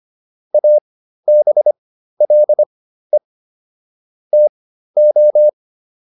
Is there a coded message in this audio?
Yes: Morse code from 0.5 to 5.5 s, heard as a keyed tone, about 25 wpm, spaced as for 13 wpm, 600 Hz. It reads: ABLE TO